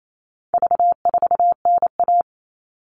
Morse code, heard as a keyed tone, about 28 wpm, 700 Hz. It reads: V4NA
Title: All Cut Numbers - 4-Digit 28wpm